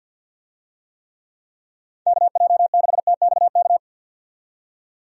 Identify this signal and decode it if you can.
Morse code, heard as a keyed tone, about 51 wpm, 700 Hz. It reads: KJ6TXK
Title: USA Call Signs - 6 Characters - 50wpm